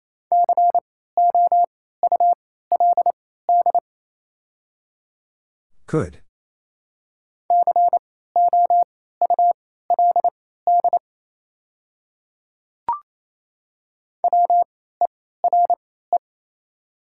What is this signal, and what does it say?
Morse code, heard as a keyed tone, about 28 wpm, spaced as for 16 wpm, 700 Hz. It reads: COULD COULD WERE
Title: 3x Character Spacing - Top 200 Words - 28wpm